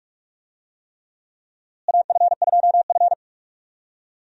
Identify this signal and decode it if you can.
Morse code, heard as a keyed tone, about 46 wpm, 700 Hz. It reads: AF2F
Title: USA Call Signs - 4 Characters - Repeated 3x - 45wpm